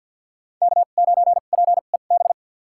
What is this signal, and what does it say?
Morse code, heard as a keyed tone, about 51 wpm, spaced as for 36 wpm, 700 Hz. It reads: K9PEB
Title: USA Call Signs - 5 Characters - 2x Character Spacing - 50wpm